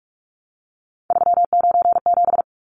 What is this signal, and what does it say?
Morse code, heard as a keyed tone, about 44 wpm, 700 Hz. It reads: 397
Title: Numbers - 3-Digit 45wpm